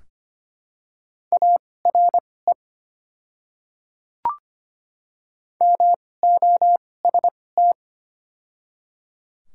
Morse code, heard as a keyed tone, about 25 wpm, spaced as for 18 wpm, 700 Hz. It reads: ARE MOST
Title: 2x Character Spacing - Top 300 Words - 25wpm